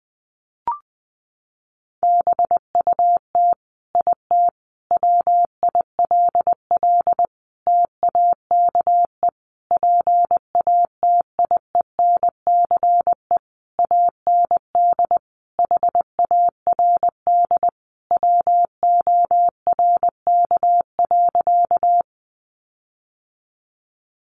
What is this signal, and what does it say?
Morse code, heard as a keyed tone, about 19 wpm, 700 Hz. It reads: BUT IT WILL TAKE PATIENCE AND HARD WORK.
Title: Sentences from Top 1300 Words 20wpm